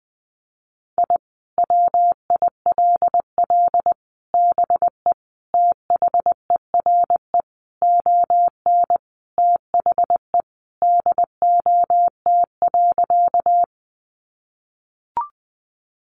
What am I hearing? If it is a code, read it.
Morse code, heard as a keyed tone, about 20 wpm, 700 Hz. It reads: I WILL BE THERE ON THE DOT.
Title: Sentences from Top 1500 Words 20wpm